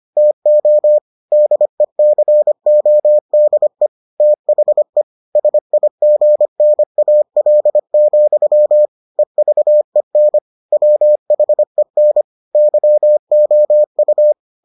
Morse code, heard as a keyed tone, about 25 wpm, 600 Hz. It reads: TO DECODE THE SIGNAL, EVEN WHEN YOU